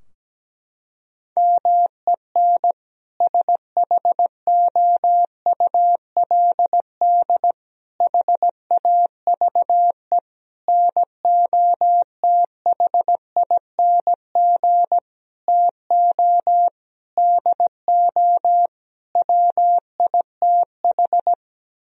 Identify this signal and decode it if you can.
Morse code, heard as a keyed tone, about 17 wpm, 700 Hz. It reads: MEN SHOULD HAVE NOTHING TO DO WITH
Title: Sentences from Top 1000 Words - Review Part 3 - 17wpm